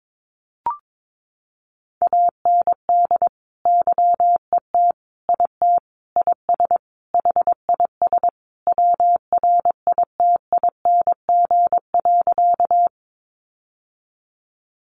Morse code, heard as a keyed tone, about 22 wpm, 700 Hz. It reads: AND YET IT IS HIS WRITING.
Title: Sentences from Top 800 Words 22wpm